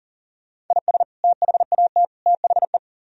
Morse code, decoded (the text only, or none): IS THAT THE